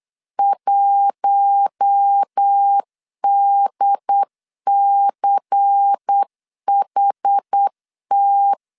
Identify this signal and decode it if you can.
Morse code, heard as a keyed tone, about 8 wpm, 800 Hz. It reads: 1DCHT
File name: LEZIONI / 7 GF9Z / 7 G 8wpm 3118